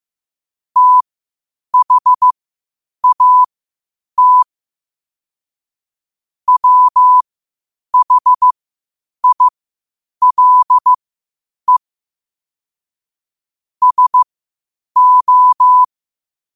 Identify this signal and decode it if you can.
Morse code, heard as a keyed tone, about 15 wpm, spaced as for 8 wpm, 1000 Hz. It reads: THAT WHILE SO
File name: qotdfiles / 8wpm